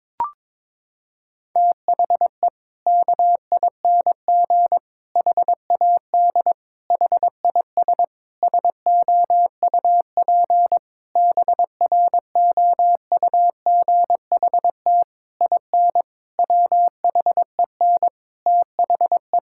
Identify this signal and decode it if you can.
Morse code, heard as a keyed tone, about 23 wpm, 700 Hz. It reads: THE KING HAD HIS SOUP BROUGHT IN WHEN THE
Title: Sentences from Top 1000 Words - Review Part 6 - 22wpm